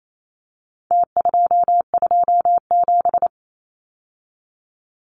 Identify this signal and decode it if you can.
Morse code, heard as a keyed tone, about 28 wpm, 700 Hz. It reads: T227
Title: Numbers - Contest Style - 4-Digit 28wpm